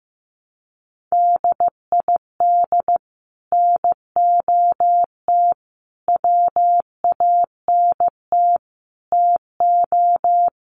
Morse code, heard as a keyed tone, about 15 wpm, 700 Hz. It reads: DID NOT WANT TO
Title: Sets of 5 Words 15wpm